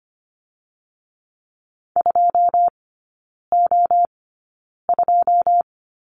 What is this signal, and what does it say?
Morse code, heard as a keyed tone, about 25 wpm, spaced as for 9 wpm, 690 Hz. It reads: 2O2
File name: Lesson 14e - Number 2 - Three Characters 25wpm